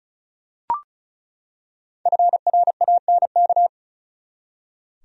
Morse code, heard as a keyed tone, about 36 wpm, 700 Hz. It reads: FRANK
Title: USA Common Names - 35wpm